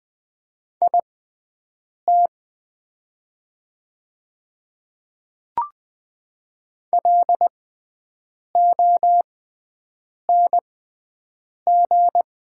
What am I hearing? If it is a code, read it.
Morse code, heard as a keyed tone, about 20 wpm, spaced as for 7 wpm, 690 Hz. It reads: IT LONG
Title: 6x Character Spacing - Top 100 Words - 20wpm